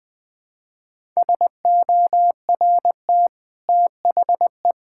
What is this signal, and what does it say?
Morse code, heard as a keyed tone, about 20 wpm, 700 Hz. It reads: SORT THE
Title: Easy Sentences with Hints - 20wpm